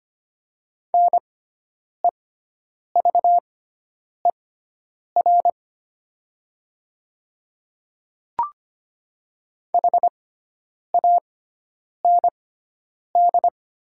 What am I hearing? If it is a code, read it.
Morse code, heard as a keyed tone, about 25 wpm, spaced as for 9 wpm, 700 Hz. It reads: NEVER HAND
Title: 6x Character Spacing - Top 500 Words - 25wpm